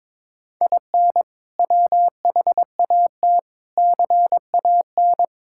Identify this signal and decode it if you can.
Morse code, heard as a keyed tone, about 22 wpm, 700 Hz. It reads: IN WHAT CAN
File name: practice / 22wpm / Sets of 8 Words - 22wpm